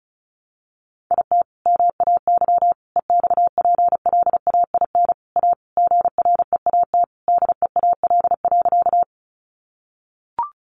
Morse code, heard as a keyed tone, about 34 wpm, 700 Hz. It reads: IT MAY EXPLAIN A GREAT DEAL.
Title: Sentences from Top 1100 Words 35wpm